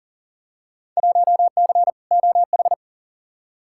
Morse code, heard as a keyed tone, about 40 wpm, 700 Hz. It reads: 1C OH
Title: Field Day Exchanges - 40wpm